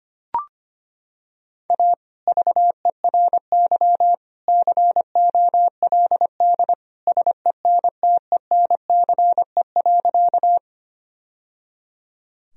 Morse code, heard as a keyed tone, about 25 wpm, 700 Hz. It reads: A VERY COLD SENTENCE.
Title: Sentences from Top 1000 Words - Review Part 1 - 25wpm